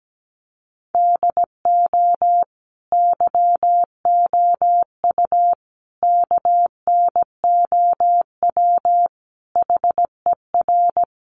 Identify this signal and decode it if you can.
Morse code, heard as a keyed tone, about 17 wpm, 700 Hz. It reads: DO YOU KNOW HER
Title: Sentences from Top 1400 Words 17wpm